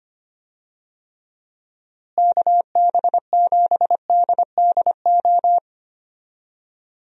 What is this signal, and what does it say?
Morse code, heard as a keyed tone, about 25 wpm, 700 Hz. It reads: KB7DDO
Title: USA Call Signs - 6 Characters - Repeated 2x - 25wpm